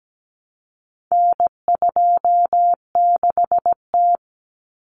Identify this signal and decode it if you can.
Morse code, heard as a keyed tone, about 17 wpm, 700 Hz. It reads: N26T
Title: Common Cut Numbers - 4-Digit 17wpm